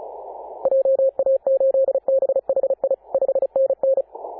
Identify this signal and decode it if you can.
Morse code, heard as a keyed tone, about 34 wpm, 540 Hz. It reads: JA8BHI 5NN
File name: Apr/29 2042z JD1BLY 14.011MHz CW